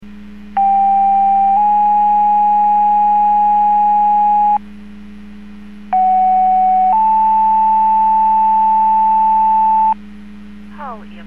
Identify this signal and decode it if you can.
Two-tone page, heard as a one-second tone, then a three-second tone, about 790 then 830 Hz, about 750 then 880 Hz, with speech. It is heard from 0.6 to 9.9 s.